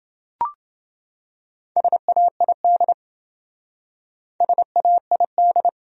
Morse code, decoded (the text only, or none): SAID SAID